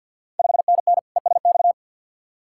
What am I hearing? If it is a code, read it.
Morse code, heard as a keyed tone, about 51 wpm, 690 Hz. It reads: 5NN ESX
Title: Ontario QSO Party Exchange 50wpm